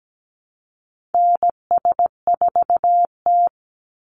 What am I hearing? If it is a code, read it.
Morse code, heard as a keyed tone, about 17 wpm, 700 Hz. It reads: NS4T
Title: Call Signs Repeated 3x - 17wpm